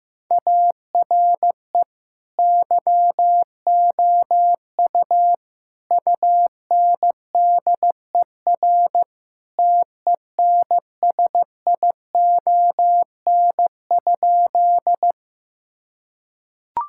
Morse code, heard as a keyed tone, about 15 wpm, 700 Hz. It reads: ARE YOU UNDER TENSION?
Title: Sentences from Top 1600 Words 15wpm